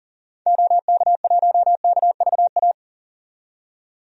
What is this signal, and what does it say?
Morse code, heard as a keyed tone, about 40 wpm, 700 Hz. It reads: OK1KVA
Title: DXCC-100 Call Signs - 6 Characters - 40wpm